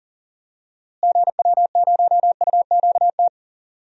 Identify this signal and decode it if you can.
Morse code, heard as a keyed tone, about 40 wpm, 700 Hz. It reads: GW0UQT
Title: DXCC-100 Call Signs - 6 Characters - Repeated 2x - 40wpm